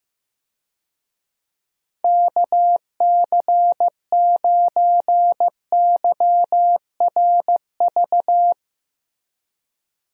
Morse code, heard as a keyed tone, about 15 wpm, 700 Hz. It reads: KC9YRV
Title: USA Call Signs - 6 Characters - Repeated 2x - 15wpm